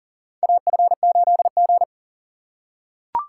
Morse code, heard as a keyed tone, about 40 wpm, 700 Hz. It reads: AF8G
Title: USA Call Signs - 3x Character Spacing - 40wpm